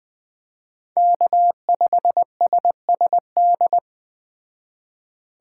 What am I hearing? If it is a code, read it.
Morse code, heard as a keyed tone, about 20 wpm, 700 Hz. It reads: K5SSD